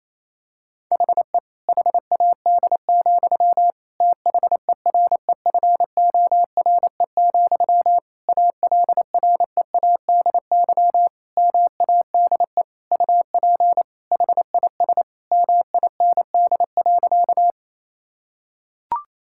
Morse code, heard as a keyed tone, about 28 wpm, 700 Hz. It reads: HE HAD, THEREFORE, ALREADY MADE UP HIS MIND.
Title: Sentences from Top 1000 Words - Review Part 5 - 28wpm